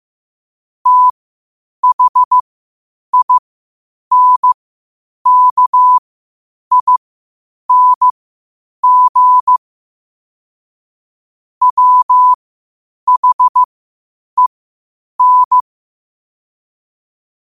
Morse code, heard as a keyed tone, about 15 wpm, spaced as for 8 wpm, 1000 Hz. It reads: THINKING WHEN
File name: qotdfiles / 8wpm